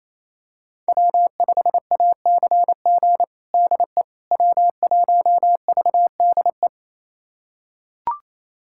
Morse code, heard as a keyed tone, about 27 wpm, 700 Hz. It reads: W5ACG DE W1VDE
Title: USA Callsign DE Callsign - 5 Characters - 28wpm